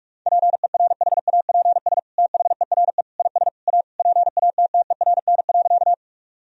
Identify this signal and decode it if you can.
Morse code, heard as a keyed tone, about 41 wpm, 700 Hz. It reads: PERHAPS THERE IS A PATTERN.
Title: Sentences from Top 1000 Words - Review Part 2 - 45wpm